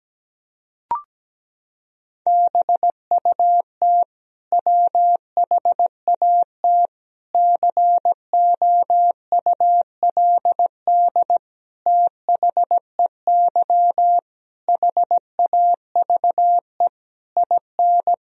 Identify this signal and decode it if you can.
Morse code, heard as a keyed tone, about 17 wpm, 700 Hz. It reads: BUT WHAT COULD THEY HAVE IN
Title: Sentences from Top 1000 Words - Review Part 3 - 17wpm